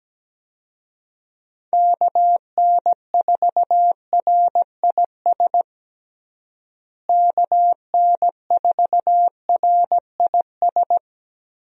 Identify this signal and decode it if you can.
Morse code, heard as a keyed tone, about 17 wpm, 700 Hz. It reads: KN4RIS KN4RIS
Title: USA Call Signs - 6 Characters - Repeated 3x - 17wpm